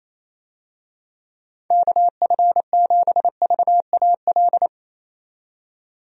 Morse code, heard as a keyed tone, about 27 wpm, 700 Hz. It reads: KF7VAL
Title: USA Call Signs - 6 Characters - 28wpm